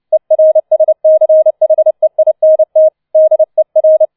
Morse code, heard as a keyed tone, about 29 wpm, 630 Hz. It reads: ERSCHEINT DER
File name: 006 Sechster Streich 30wpm